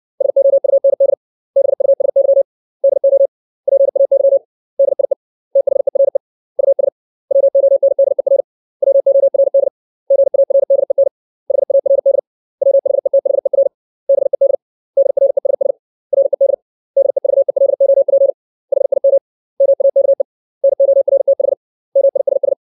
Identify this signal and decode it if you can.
Morse code, heard as a keyed tone, about 60 wpm, 550 Hz. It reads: SOUND BUSY DO PAY DIE THERE US WONDER WORD GARDEN HARD WHETHER BED DRESS RED DEVELOP HIM MAKE NORTH MISS
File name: 632 words / 632 Words 60WPM